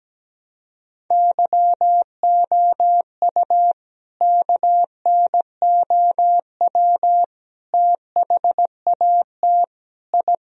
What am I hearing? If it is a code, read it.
Morse code, heard as a keyed tone, about 17 wpm, 700 Hz. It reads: YOU KNOW THAT I
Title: Sentences from Top 1000 Words - Review Part 1 - 17wpm